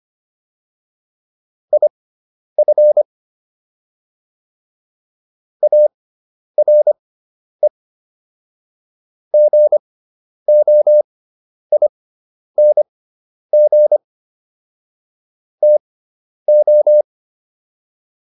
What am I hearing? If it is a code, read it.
Morse code, heard as a keyed tone, about 25 wpm, spaced as for 10 wpm, 600 Hz. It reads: IF ARE GOING TO